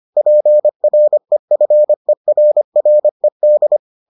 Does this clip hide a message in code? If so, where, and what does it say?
Yes: Morse code from 0.2 to 3.8 s, heard as a keyed tone, about 25 wpm, 600 Hz. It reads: PREFERRED